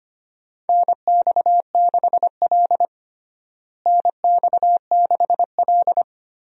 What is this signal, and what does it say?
Morse code, heard as a keyed tone, about 25 wpm, 700 Hz. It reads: NX6L NX6L
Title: USA Call Signs - 4 Characters - Repeated 2x - 25wpm